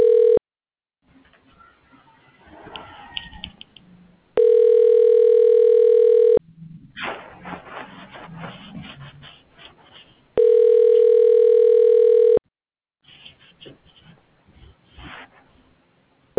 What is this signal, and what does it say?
Ringback tone (North America), 440 + 480 Hz.